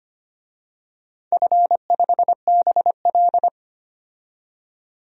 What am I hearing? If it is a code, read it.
Morse code, heard as a keyed tone, about 25 wpm, 700 Hz. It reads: F5BL
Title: DXCC-100 Call Signs - 4 Characters - 25wpm